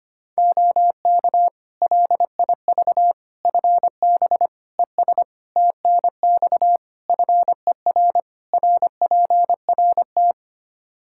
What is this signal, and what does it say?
Morse code, heard as a keyed tone, about 25 wpm, 700 Hz. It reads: OK LIV FB ES TNX FER RPRT
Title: Practice QSO - Exchange 2 - Info - 25wpm